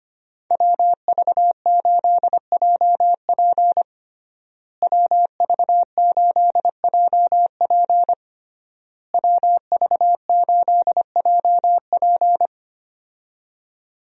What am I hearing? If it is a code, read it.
Morse code, heard as a keyed tone, about 25 wpm, 700 Hz. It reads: WV8JP WV8JP WV8JP
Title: USA Call Signs - 5 Characters - Repeated 3x - 25wpm